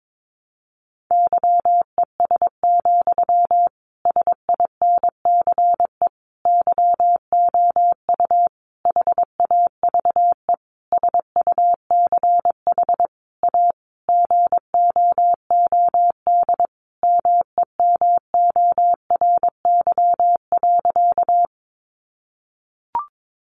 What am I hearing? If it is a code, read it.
Morse code, heard as a keyed tone, about 22 wpm, 700 Hz. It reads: YES, SINCE YOU HAVE SUCH A GOOD MEMORY.
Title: Sentences from Top 1300 Words 22wpm